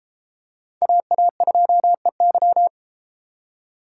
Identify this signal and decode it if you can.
Morse code, heard as a keyed tone, about 33 wpm, 700 Hz. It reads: AA2EY